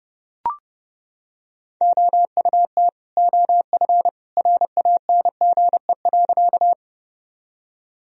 Morse code, heard as a keyed tone, about 32 wpm, 700 Hz. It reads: OUT OF RANGE.
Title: Sentences from Top 1000 Words - Review Part 4 - 30wpm